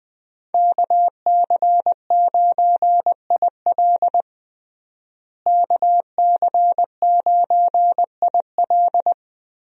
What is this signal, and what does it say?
Morse code, heard as a keyed tone, about 20 wpm, 700 Hz. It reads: KC9IL KC9IL
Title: USA Call Signs - 5 Characters - Repeated 2x - 20wpm